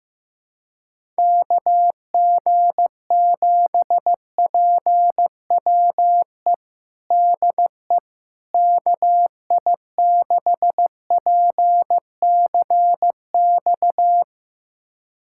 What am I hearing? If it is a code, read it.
Morse code, heard as a keyed tone, about 15 wpm, 700 Hz. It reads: KG7PWE DE KI6PCX
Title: USA Callsign DE Callsign - 6 Characters - 15wpm